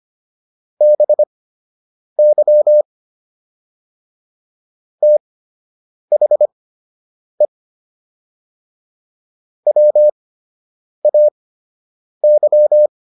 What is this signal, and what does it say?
Morse code, heard as a keyed tone, about 25 wpm, spaced as for 8 wpm, 600 Hz. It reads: BY THE WAY